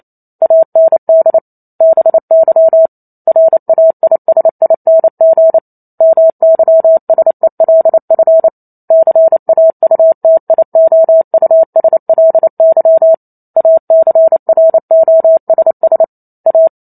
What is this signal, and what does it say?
Morse code, heard as a keyed tone, about 28 wpm, 650 Hz. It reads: AND BY RAISING MYSELF CAUTIOUSLY ACROSS A